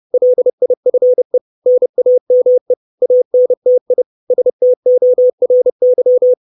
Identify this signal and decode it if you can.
Morse code, heard as a keyed tone, about 30 wpm, 500 Hz. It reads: LIFE NAME ANTI STORY
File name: 632 Words 30WPM